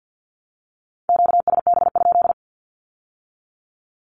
Morse code, heard as a keyed tone, about 50 wpm, 700 Hz. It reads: QSB?
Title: Q-Codes with Spoken Meaning - 50wpm